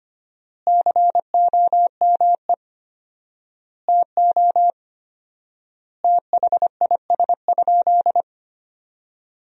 Morse code, heard as a keyed tone, about 25 wpm, 700 Hz. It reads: COME TO THIS?
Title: 4x Word-Spacing - Sentences from Top 100 Words - 25wpm